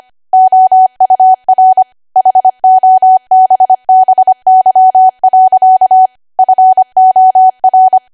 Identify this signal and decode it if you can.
Morse code, heard as a keyed tone, about 25 wpm, 750 Hz. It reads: OUR HOBBY. FOR
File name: Archive / 25 WPM / 25WPM